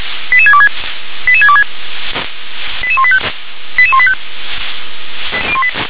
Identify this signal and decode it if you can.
Five-tone selective call (ZVEI), heard as quick runs of five tones, tones of about 70 ms. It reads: 88626 88626 88185 88185
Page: /357/SDR/ZVEI/